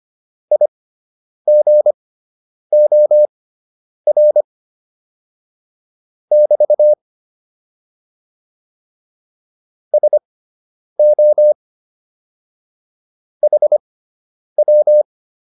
Morse code, heard as a keyed tone, about 25 wpm, spaced as for 9 wpm, 600 Hz. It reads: IGOR <BT> SO HW